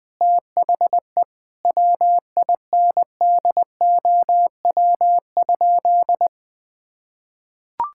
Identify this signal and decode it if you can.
Morse code, heard as a keyed tone, about 20 wpm, 700 Hz. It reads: THE WINDOW?
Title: Sentences from Top 1600 Words 20wpm